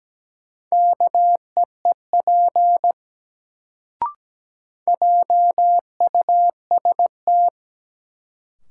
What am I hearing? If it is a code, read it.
Morse code, heard as a keyed tone, about 17 wpm, 700 Hz. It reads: KEEP JUST
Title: Sets of 2 Words - New Word Drill - 17wpm